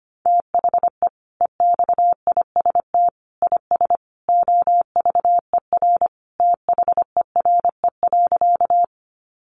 Morse code, heard as a keyed tone, about 25 wpm, 700 Hz. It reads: THE EXIST IS OVER THERE.
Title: Sentences from Top 1100 Words 25wpm